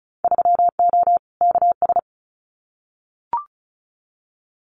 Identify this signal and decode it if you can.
Morse code, heard as a keyed tone, about 36 wpm, 700 Hz. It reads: 3O KS
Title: Winter Field Day Exchanges Encore 35wpm